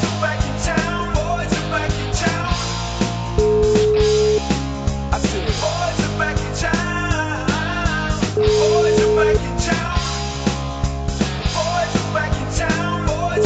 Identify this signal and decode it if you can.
Ringback tone (Europe, 425 Hz), with noise.